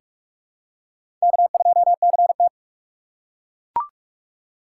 Morse code, heard as a keyed tone, about 43 wpm, 700 Hz. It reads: K2CT
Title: USA Call Signs - 4 Characters - 2x Character Spacing - 45wpm